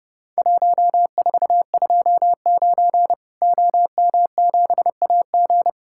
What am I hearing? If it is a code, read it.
Morse code, heard as a keyed tone, about 30 wpm, 700 Hz. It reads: 1429 OM7AG
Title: Worked All Europe DX Contest - QTC Exchange - 30wpm